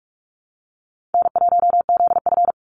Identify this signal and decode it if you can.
Morse code, heard as a keyed tone, about 44 wpm, 700 Hz. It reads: N1ZF
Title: NA SOTA Chaser Callsigns 45wpm